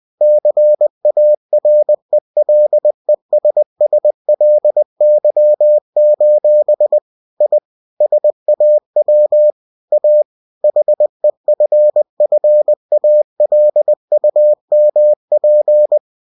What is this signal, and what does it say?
Morse code, heard as a keyed tone, about 20 wpm, 600 Hz. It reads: CARELESSLY: I SAW A HEFFALUMP
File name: cw-books / winnie-the-pooh / winnie-20wpm-page-0033